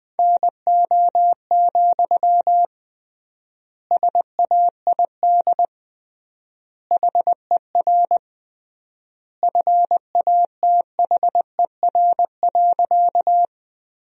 Morse code, heard as a keyed tone, about 20 wpm, 700 Hz. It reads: NO, SAID HER FATHER.
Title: 3x Word-Spacing - Sentences from Top 300 Words - 20wpm